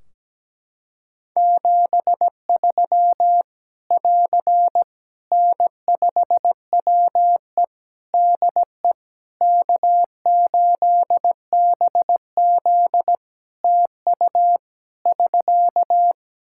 Morse code, heard as a keyed tone, about 17 wpm, 700 Hz. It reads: 73 <AR> N5WE DE K8BZ TU <SK>
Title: K1USN SST QSO Practice 17wpm